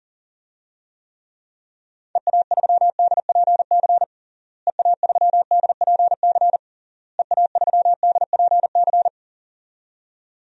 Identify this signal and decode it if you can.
Morse code, heard as a keyed tone, about 40 wpm, 700 Hz. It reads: EA3DPC EA3DPC EA3DPC
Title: DXCC-100 Call Signs - 6 Characters - Repeated 3x - 40wpm